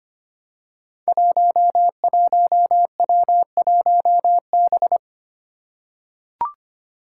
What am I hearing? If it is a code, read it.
Morse code, heard as a keyed tone, about 25 wpm, 700 Hz. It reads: 11W1B
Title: Five Challenging Letter-Number Combinations 25wpm